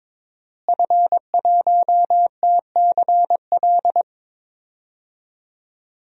Morse code, heard as a keyed tone, about 21 wpm, 700 Hz. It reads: F1TCL
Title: DXCC-100 Call Signs - 5 Characters - Repeated 3x - 22wpm